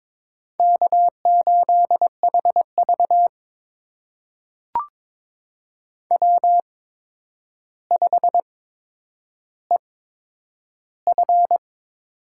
Morse code, heard as a keyed tone, about 22 wpm, spaced as for 9 wpm, 700 Hz. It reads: K8HV W5EF
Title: USA Call Signs - 4 Characters - 8x Character Spacing - 22wpm